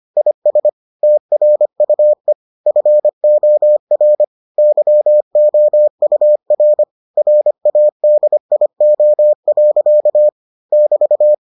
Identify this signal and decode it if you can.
Morse code, heard as a keyed tone, about 25 wpm, 600 Hz. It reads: IS TRUE FOR YOUR RADIO. <BT>